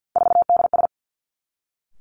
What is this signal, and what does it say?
Morse code, heard as a keyed tone, about 50 wpm, 700 Hz. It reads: 4DS
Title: Three Challenging Letter-Number Combinations 50wpm